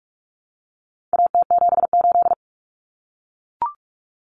Morse code, heard as a keyed tone, about 44 wpm, 700 Hz. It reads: AT78
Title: Common Cut Numbers - 4-Digit 45wpm